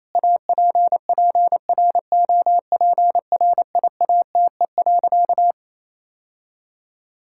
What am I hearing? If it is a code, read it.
Morse code, heard as a keyed tone, about 28 wpm, 700 Hz. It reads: APPROPRIATE.
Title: Sentences from Top 1000 Words - Review Part 5 - 28wpm